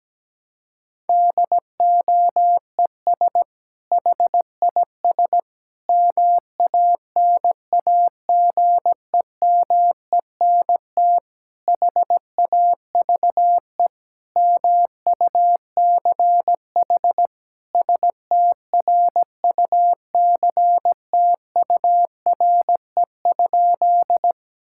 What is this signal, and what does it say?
Morse code, heard as a keyed tone, about 17 wpm, 700 Hz. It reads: DOES HIS MANAGEMENT HAVE MUCH STRUCTURE?
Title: Sentences from Top 1000 Words - Review Part 6 - 17wpm